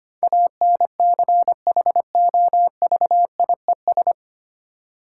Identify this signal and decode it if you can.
Morse code, heard as a keyed tone, about 25 wpm, 700 Hz. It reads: ANCHOVIES
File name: practice / 25wpm / Difficult Words with Hints - 25wpm